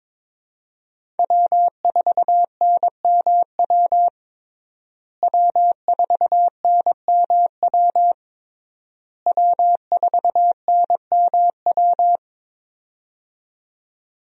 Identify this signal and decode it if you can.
Morse code, heard as a keyed tone, about 22 wpm, 700 Hz. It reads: W4NMW W4NMW W4NMW
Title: USA Call Signs - 5 Characters - Repeated 3x - 22wpm